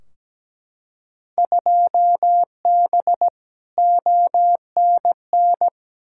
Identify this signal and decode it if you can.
Morse code, heard as a keyed tone, about 17 wpm, 700 Hz. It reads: 2B ONN
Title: Field Day Exchanges Encore - 17wpm